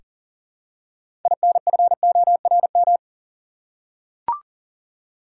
Morse code, heard as a keyed tone, about 40 wpm, 700 Hz. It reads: INFORM
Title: Six Letter Words - 40wpm